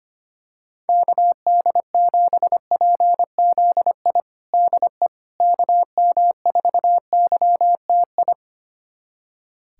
Morse code, heard as a keyed tone, about 25 wpm, 700 Hz. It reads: KD7PZI DE KM4YTI
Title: USA Callsign DE Callsign - 6 Characters - 25wpm